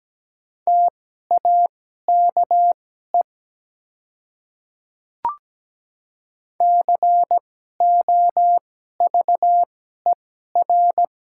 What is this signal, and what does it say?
Morse code, heard as a keyed tone, about 17 wpm, spaced as for 12 wpm, 700 Hz. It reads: TAKE COVER
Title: 2x Character Spacing - Top 300 Words - 17wpm